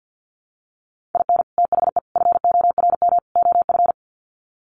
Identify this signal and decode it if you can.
Morse code, heard as a keyed tone, about 50 wpm, 700 Hz. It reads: IN THE FORM OF
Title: Sets of 4 Words 50wpm